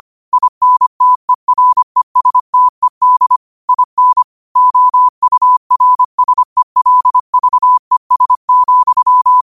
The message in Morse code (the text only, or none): INTERESTED IN OURSELVES,